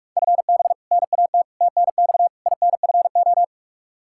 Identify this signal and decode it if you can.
Morse code, heard as a keyed tone, about 47 wpm, 700 Hz. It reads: FB NAT TNX INFO
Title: Practice QSO - Exchange 3 - More Info - 45wpm